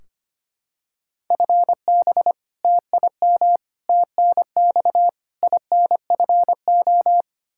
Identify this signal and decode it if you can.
Morse code, heard as a keyed tone, about 25 wpm, 700 Hz. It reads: FB TIM TNX INFO
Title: Practice QSO - Exchange 3 - More Info - 25wpm